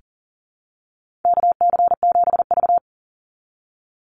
Morse code, heard as a keyed tone, about 40 wpm, 700 Hz. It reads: KC7V
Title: Call Signs - 40wpm